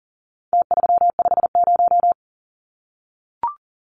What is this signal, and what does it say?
Morse code, heard as a keyed tone, about 40 wpm, 700 Hz. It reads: T350
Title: Numbers - Contest Style - 4-Digit 40wpm